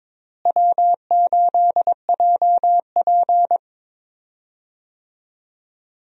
Morse code, heard as a keyed tone, about 21 wpm, 700 Hz. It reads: W8JP
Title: USA Call Signs - 4 Characters - Repeated 3x - 22wpm